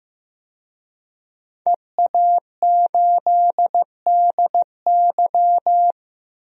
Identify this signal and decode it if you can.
Morse code, heard as a keyed tone, about 15 wpm, 700 Hz. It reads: EA8DY
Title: DXCC-100 Call Signs - 5 Characters - 15wpm